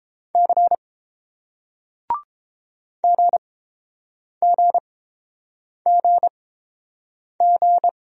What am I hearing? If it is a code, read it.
Morse code, heard as a keyed tone, about 29 wpm, spaced as for 7 wpm, 700 Hz. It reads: C GGGG